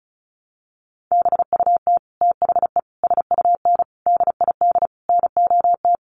Morse code, heard as a keyed tone, about 34 wpm, 700 Hz. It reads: BUT THE SUN DID NOT
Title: Sentences from Top 1100 Words 35wpm